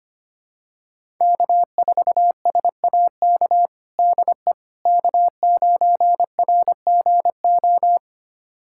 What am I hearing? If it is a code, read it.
Morse code, heard as a keyed tone, about 25 wpm, 700 Hz. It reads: K4SAK DE K9RGO
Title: USA Callsign DE Callsign - 5 Characters - 25wpm